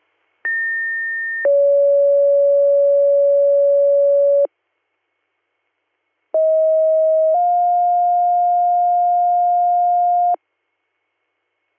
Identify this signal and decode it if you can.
Two-tone page, heard as a one-second tone, then a three-second tone, about 1800 then 560 Hz, about 640 then 710 Hz. It is heard from 0.4 to 10.3 s.